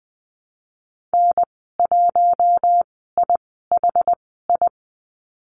Morse code, heard as a keyed tone, about 20 wpm, spaced as for 14 wpm, 700 Hz. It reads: N1IHI
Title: 062 Lesson 11g - Letter h - Call Signs 20wpm